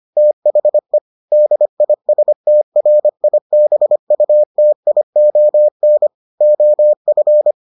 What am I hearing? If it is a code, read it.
Morse code, heard as a keyed tone, about 25 wpm, 600 Hz. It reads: THE DISTRIBUTION OF